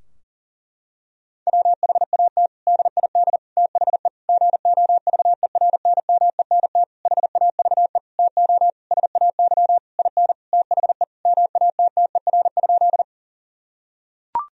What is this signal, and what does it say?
Morse code, heard as a keyed tone, about 40 wpm, 700 Hz. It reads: WHAT DID THE GOVERNMENT HAVE TO SAY IN THE MATTER?
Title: Sentences from Top 1000 Words - Review Part 3 - 40wpm